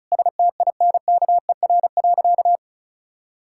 Morse code, heard as a keyed tone, about 36 wpm, 700 Hz. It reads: STINKER.
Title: Pangram Challenge v2 - 35wpm